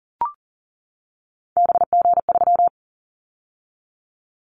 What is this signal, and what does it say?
Morse code, heard as a keyed tone, about 38 wpm, 700 Hz. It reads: BG3
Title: Callsign Prefixes for Top 100 DXCC - 40wpm